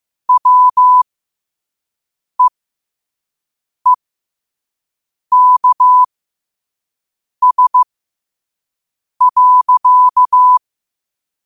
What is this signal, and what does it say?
Morse code, heard as a keyed tone, about 15 wpm, spaced as for 5 wpm, 1000 Hz. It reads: WEEKS.
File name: qotdfiles / 5wpm